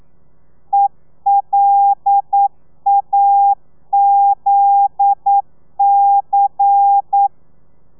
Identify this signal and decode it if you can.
Morse code, heard as a keyed tone, about 9 wpm, 800 Hz. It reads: ELAZC